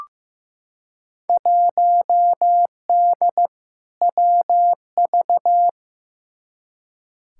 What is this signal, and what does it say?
Morse code, heard as a keyed tone, about 15 wpm, 700 Hz. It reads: 1D WV
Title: Field Day Exchanges - 15wpm